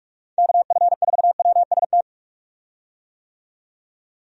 Morse code, heard as a keyed tone, about 46 wpm, 700 Hz. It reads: KF4WST
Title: USA Call Signs - 6 Characters - 45wpm